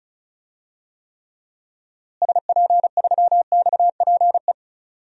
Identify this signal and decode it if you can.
Morse code, heard as a keyed tone, about 36 wpm, 700 Hz. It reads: SP3XPE
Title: DXCC-100 Call Signs - 6 Characters - Repeated 2x - 35wpm